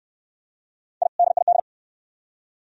Morse code, heard as a keyed tone, about 65 wpm, 690 Hz. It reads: I DID